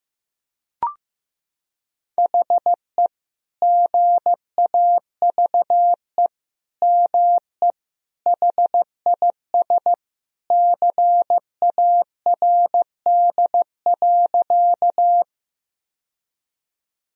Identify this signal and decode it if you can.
Morse code, heard as a keyed tone, about 16 wpm, 700 Hz. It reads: HE GAVE ME HIS CARD.
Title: Sentences from Top 1000 Words - Review Part 4 - 15wpm